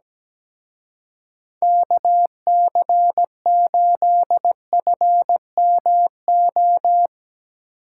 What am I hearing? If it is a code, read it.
Morse code, heard as a keyed tone, about 19 wpm, 700 Hz. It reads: KC8FMO
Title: Speed Racing - USA 2x3 Call Signs 17wpm